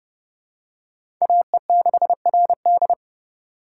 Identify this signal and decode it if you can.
Morse code, heard as a keyed tone, about 30 wpm, 700 Hz. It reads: AE6RD
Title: USA Call Signs - 5 Characters - 2x Character Spacing - 30wpm